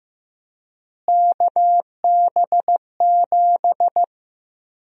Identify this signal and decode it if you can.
Morse code, heard as a keyed tone, about 15 wpm, 700 Hz. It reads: KB7
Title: USA Call Sign Prefixes - Repeated 2x - 15wpm